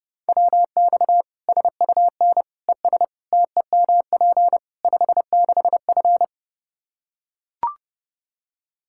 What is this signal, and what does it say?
Morse code, heard as a keyed tone, about 32 wpm, 700 Hz. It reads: WX SUN ES TEMP 56F
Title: Practice QSO - Exchange 2 - Info - 30wpm